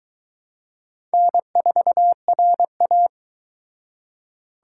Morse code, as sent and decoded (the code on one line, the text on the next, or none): -. ....- .-. .-
N4RA